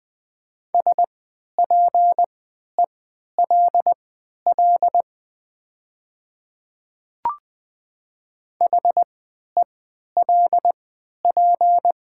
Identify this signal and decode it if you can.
Morse code, heard as a keyed tone, about 20 wpm, spaced as for 11 wpm, 700 Hz. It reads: SPELL HELP
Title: 3x Character Spacing - Top 200 Words - 20wpm